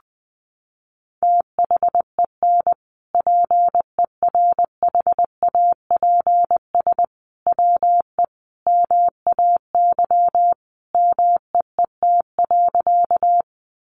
Morse code, heard as a keyed tone, about 20 wpm, 700 Hz. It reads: THEN PERHAPS WE MAY MEET.
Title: Sentences from Top 1100 Words 20wpm